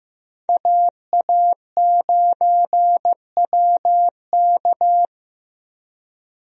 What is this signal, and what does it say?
Morse code, heard as a keyed tone, about 15 wpm, 700 Hz. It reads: AA9WK
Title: USA Call Signs - 5 Characters - Repeated 3x - 15wpm